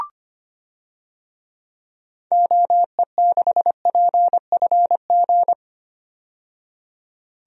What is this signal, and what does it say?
Morse code, heard as a keyed tone, about 25 wpm, 700 Hz. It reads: OE6PFG
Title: DXCC-100 Call Signs - 6 Characters - 25wpm